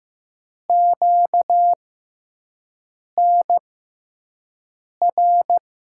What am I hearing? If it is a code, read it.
Morse code, heard as a keyed tone, about 15 wpm, spaced as for 5 wpm, 700 Hz. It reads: QNR
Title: USA Call Signs - 6 Characters - 6x Character Spacing - 15wpm